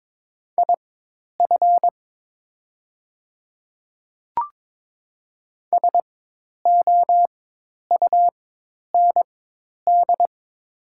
Morse code, heard as a keyed tone, about 22 wpm, spaced as for 10 wpm, 700 Hz. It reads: IF SOUND
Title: 4x Character Spacing - Top 100 Words - 22wpm